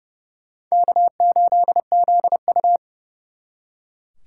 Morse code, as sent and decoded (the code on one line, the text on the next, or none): -.- ---.. --.. ..-
K8ZU